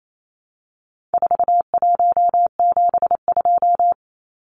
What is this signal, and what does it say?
Morse code, heard as a keyed tone, about 28 wpm, 700 Hz. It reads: 4172
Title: Numbers - 4-Digit 28wpm